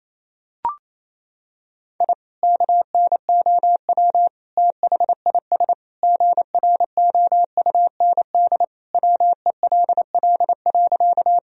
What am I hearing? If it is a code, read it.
Morse code, heard as a keyed tone, about 27 wpm, 700 Hz. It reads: I KNOW THIS GROUND WELL.
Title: Sentences from Top 1000 Words - Review Part 2 - 28wpm